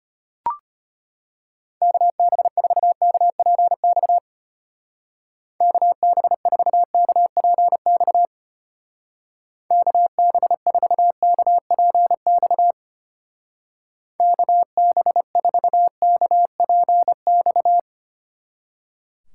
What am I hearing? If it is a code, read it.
Morse code, heard as a keyed tone, about 32 wpm, 700 Hz. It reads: KB4KPX KB4KPX KB4KPX KB4KPX